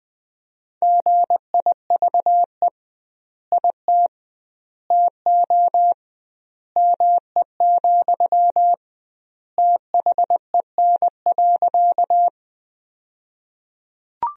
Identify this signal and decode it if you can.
Morse code, heard as a keyed tone, about 20 wpm, 700 Hz. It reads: GIVE IT TO ME, THEN.
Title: 2x Word-Spacing - Sentences from Top 200 Words - 20wpm